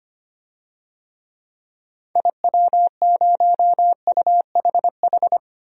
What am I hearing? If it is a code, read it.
Morse code, heard as a keyed tone, about 25 wpm, 700 Hz. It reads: IW0UHH
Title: DXCC-100 Call Signs - 6 Characters - 25wpm